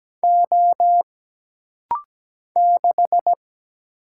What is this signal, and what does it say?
Morse code, heard as a keyed tone, about 17 wpm, 700 Hz. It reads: O 6
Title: Single Letter-Number Alt - Rapid-Fire 17wpm